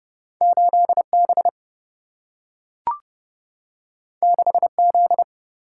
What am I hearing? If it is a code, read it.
Morse code, heard as a keyed tone, about 31 wpm, 700 Hz. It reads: 8B 6Z
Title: Single-Letter Single-Number 30wpm